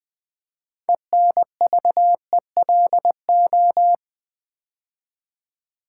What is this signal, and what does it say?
Morse code, heard as a keyed tone, about 20 wpm, 700 Hz. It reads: ENVELO
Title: Build A Word - 2 Letters At A Time - 8-Letter Words - 20wpm